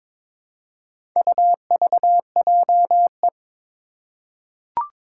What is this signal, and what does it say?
Morse code, heard as a keyed tone, about 22 wpm, 700 Hz. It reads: UVJE
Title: Four Challenging Letter-Number Combinations 22wpm